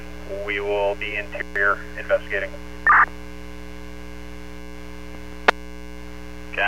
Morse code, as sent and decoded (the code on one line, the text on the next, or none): none